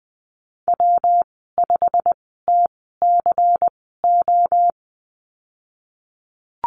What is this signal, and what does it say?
Morse code, heard as a keyed tone, about 20 wpm, spaced as for 15 wpm, 700 Hz. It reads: W5TCO
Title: 167 Lesson 27g - Number 7 - Call Signs 20wpm